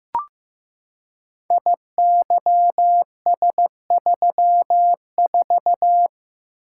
Morse code, heard as a keyed tone, about 15 wpm, 700 Hz. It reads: IYS34
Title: Code Groups - 5 Letter:Numbers - 15wpm